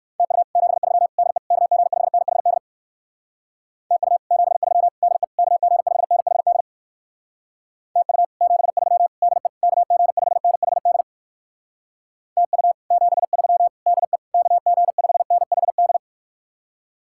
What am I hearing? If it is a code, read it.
Morse code, heard as a keyed tone, about 59 wpm, 700 Hz. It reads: TU 73 DE KG5NHD TU 73 DE KG5NHD TU 73 DE KG5NHD TU 73 DE KG5NHD